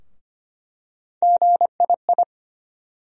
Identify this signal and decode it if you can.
Morse code, heard as a keyed tone, about 25 wpm, 700 Hz. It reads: GII
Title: USA Call Sign Postfixes - 3 Letters 25wpm